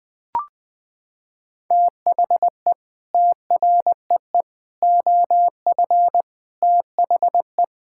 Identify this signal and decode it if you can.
Morse code, heard as a keyed tone, about 20 wpm, 700 Hz. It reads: THE TREE OF THE
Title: Sentences with 9-Word Phrases - 20wpm